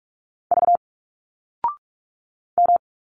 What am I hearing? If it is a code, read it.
Morse code, heard as a keyed tone, about 42 wpm, 700 Hz. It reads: V M
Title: Single Letters 45wpm